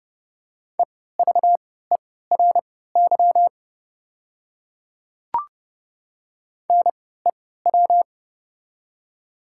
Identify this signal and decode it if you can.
Morse code, heard as a keyed tone, about 30 wpm, spaced as for 17 wpm, 700 Hz. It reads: EVERY NEW
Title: 3x Character Spacing - Top 200 Words - 30wpm